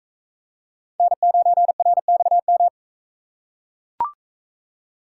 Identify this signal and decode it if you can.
Morse code, heard as a keyed tone, about 42 wpm, 700 Hz. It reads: N9RXM